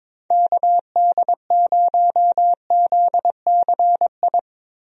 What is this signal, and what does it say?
Morse code, heard as a keyed tone, about 22 wpm, 700 Hz. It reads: KD0ZCI
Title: USA Call Signs - 6 Characters - Repeated 3x - 22wpm